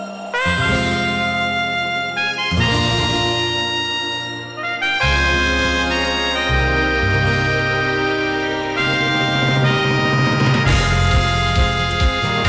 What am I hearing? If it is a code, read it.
Dial-up modem handshake.